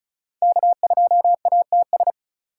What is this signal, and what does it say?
Morse code, heard as a keyed tone, about 36 wpm, 700 Hz. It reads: K2ATS
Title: USA Call Signs - 3x Character Spacing - 35wpm